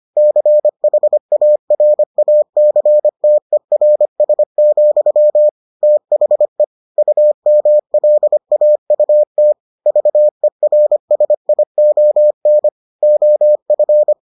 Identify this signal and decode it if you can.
Morse code, heard as a keyed tone, about 25 wpm, 600 Hz. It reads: CHARACTERS, THE UMLAUT VERSION OF